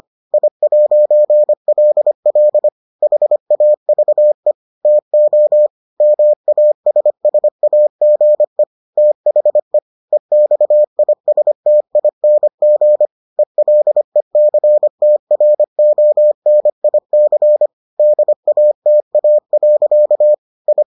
Morse code, heard as a keyed tone, about 25 wpm, 600 Hz. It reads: I'LL HAVE TO MASSAGE THE EXISTING ELECTRONIC DATA. I